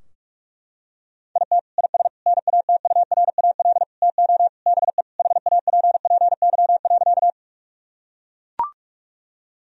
Morse code, heard as a keyed tone, about 41 wpm, 700 Hz. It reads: IT IS NATURAL TO BE HAPPY.
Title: Sentences from Top 1000 Words - Review Part 4 - 45wpm